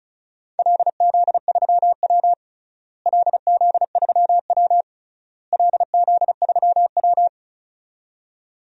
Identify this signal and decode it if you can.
Morse code, heard as a keyed tone, about 36 wpm, 700 Hz. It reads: LZ3W LZ3W LZ3W
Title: DXCC-100 Call Signs - 4 Characters - Repeated 3x - 35wpm